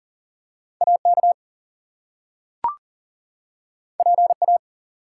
Morse code, heard as a keyed tone, about 40 wpm, 700 Hz. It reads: AK PA
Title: US States and Canadian Provinces 40wpm